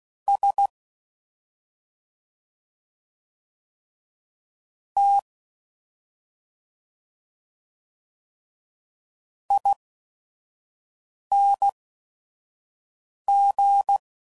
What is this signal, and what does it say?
Morse code, heard as a keyed tone, about 16 wpm, spaced as for 5 wpm, 800 Hz. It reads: S T ING
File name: Suffixes / Special ING's / Sngl ING's